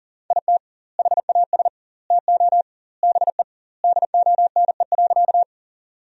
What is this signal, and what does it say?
Morse code, heard as a keyed tone, about 37 wpm, 700 Hz. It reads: IT HAS TO BE DONE.
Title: 2x Word-Spacing - Sentences from Top 500 Words - 40wpm